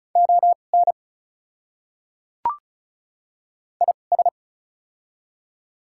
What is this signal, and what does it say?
Morse code, heard as a keyed tone, about 34 wpm, spaced as for 25 wpm, 690 Hz. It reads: ON IS
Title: 2x Character Spacing - Top 100 Words - 35wpm